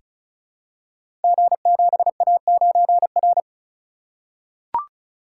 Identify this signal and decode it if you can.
Morse code, heard as a keyed tone, about 36 wpm, 700 Hz. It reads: G7A9R
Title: Code Groups - 5 Letter:Numbers - 35wpm